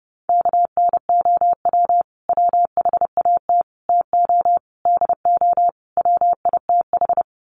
Morse code, heard as a keyed tone, about 30 wpm, 700 Hz. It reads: KNOW WHAT TO DO WITH
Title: Sets of 5 Words 30wpm